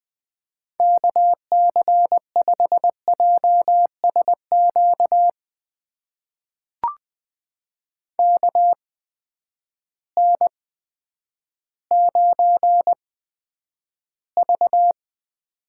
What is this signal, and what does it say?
Morse code, heard as a keyed tone, about 20 wpm, spaced as for 10 wpm, 700 Hz. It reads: KC5JSQ KN9V
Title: USA Call Signs - 8x Character Spacing - 20wpm